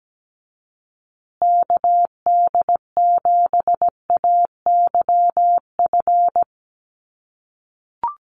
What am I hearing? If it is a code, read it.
Morse code, heard as a keyed tone, about 17 wpm, 700 Hz. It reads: KD7AYF
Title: USA Call Signs Repeated 3x 17wpm